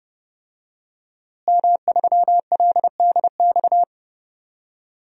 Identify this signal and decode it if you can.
Morse code, heard as a keyed tone, about 30 wpm, 700 Hz. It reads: M3LDX
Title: DXCC-100 Call Signs - 5 Characters - Repeated 2x - 30wpm